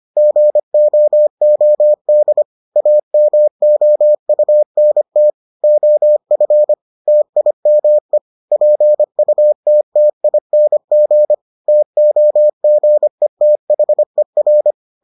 Morse code, heard as a keyed tone, about 25 wpm, 600 Hz. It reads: GOOD AMOUNT OF TIME PUTTING TOGETHER